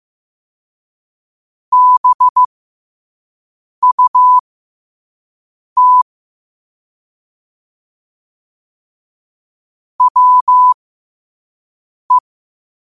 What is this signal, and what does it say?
Morse code, heard as a keyed tone, about 15 wpm, spaced as for 5 wpm, 1000 Hz. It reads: BUT WE
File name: qotdfiles / 5wpm